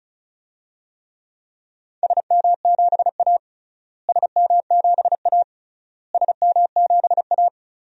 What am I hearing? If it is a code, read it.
Morse code, heard as a keyed tone, about 36 wpm, 700 Hz. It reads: SM7A SM7A SM7A
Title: DXCC-100 Call Signs - 4 Characters - Repeated 3x - 35wpm